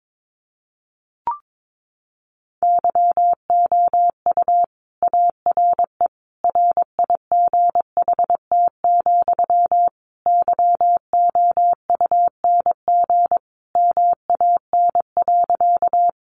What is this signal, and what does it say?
Morse code, heard as a keyed tone, about 22 wpm, 700 Hz. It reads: YOU ARE RIGHT, YOUNG MAN.
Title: Sentences from Top 400 Words 22wpm